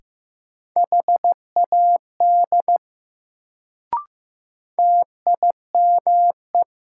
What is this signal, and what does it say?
Morse code, heard as a keyed tone, about 15 wpm, 700 Hz. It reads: HAD TIME
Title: Sentences from Top 100 Words - New Word Drill - 15wpm